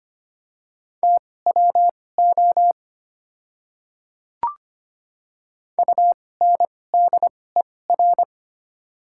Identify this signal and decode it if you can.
Morse code, heard as a keyed tone, about 25 wpm, spaced as for 18 wpm, 690 Hz. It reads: TWO UNDER
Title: 2x Character Spacing - Top 300 Words - 25wpm